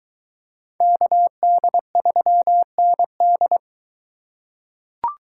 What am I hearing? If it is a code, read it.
Morse code, heard as a keyed tone, about 20 wpm, 700 Hz. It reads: KD3ND